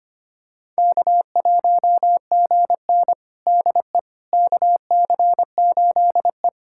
Morse code, heard as a keyed tone, about 25 wpm, 700 Hz. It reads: K1GN DE KC8E
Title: USA Callsign DE Callsign - 4 Characters - 25wpm